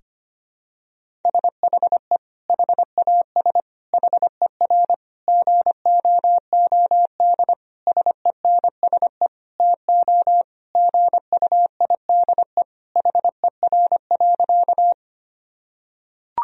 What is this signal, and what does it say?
Morse code, heard as a keyed tone, about 25 wpm, 700 Hz. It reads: SHE HAS HER GOOD SENSE TO GUIDE HER.
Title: Sentences from Top 1000 Words - Review Part 4 - 25wpm